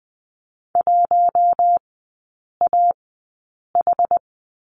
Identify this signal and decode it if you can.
Morse code, heard as a keyed tone, about 19 wpm, spaced as for 8 wpm, 690 Hz. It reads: 1AH
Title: 166 Lesson 27e - Number 7 - Three Characters 20wpm